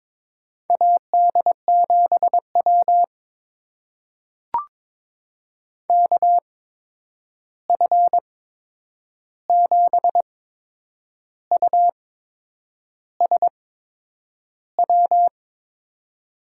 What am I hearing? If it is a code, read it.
Morse code, heard as a keyed tone, about 22 wpm, spaced as for 8 wpm, 700 Hz. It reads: AD7W KF7USW
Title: USA Call Signs - 8x Character Spacing - 22wpm